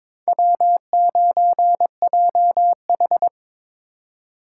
Morse code, heard as a keyed tone, about 26 wpm, 700 Hz. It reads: W9JH
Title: Speed Racing - USA 1x2 Call Signs 22wpm